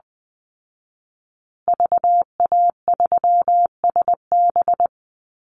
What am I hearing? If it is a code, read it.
Morse code, heard as a keyed tone, about 20 wpm, 700 Hz. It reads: VA3SB
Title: Call Signs Repeated 2x - 20wpm